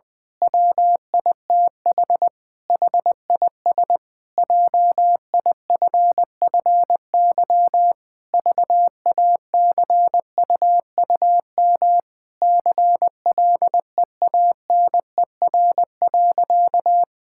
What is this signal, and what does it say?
Morse code, heard as a keyed tone, about 19 wpm, 700 Hz. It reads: WITH HIS JIFFY VACUUM CLEANER.
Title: Pangram Challenge v2 - 20wpm